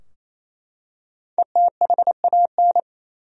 Morse code, heard as a keyed tone, about 28 wpm, 700 Hz. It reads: ETHAN
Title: USA Common Names - Max 5 Letters - 28wpm